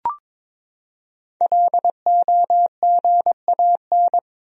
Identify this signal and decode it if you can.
Morse code, heard as a keyed tone, about 22 wpm, 700 Hz. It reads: LOGAN
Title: USA Common Names - 22wpm